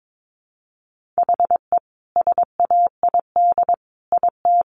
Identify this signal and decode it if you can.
Morse code, heard as a keyed tone, about 22 wpm, 700 Hz. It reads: HE SAID IT
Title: Sentences from Top 1400 Words 22wpm